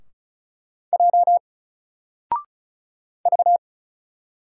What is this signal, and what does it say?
Morse code, heard as a keyed tone, about 36 wpm, 700 Hz. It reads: J V
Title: Single Letter-Number Alt 35wpm